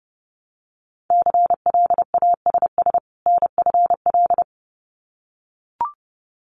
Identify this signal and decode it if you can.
Morse code, heard as a keyed tone, about 30 wpm, 700 Hz. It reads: CLASS NFL
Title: Winter Field Day Exchanges 30wpm